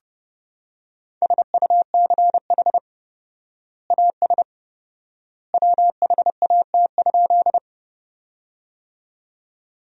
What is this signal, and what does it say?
Morse code, heard as a keyed tone, about 32 wpm, 700 Hz. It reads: SUCH AS WHAT?
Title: 4x Word-Spacing - Sentences from Top 200 Words - 30wpm